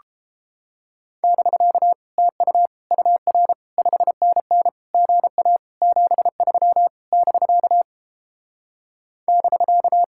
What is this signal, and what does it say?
Morse code, heard as a keyed tone, about 29 wpm, 700 Hz. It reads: <BK> TU UR 5NN GA 73 <BK> <BK>